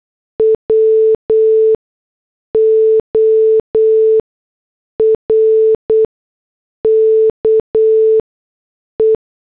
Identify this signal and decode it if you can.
Morse code, heard as a keyed tone, about 8 wpm, spaced as for 6 wpm, 440 Hz. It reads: WORKE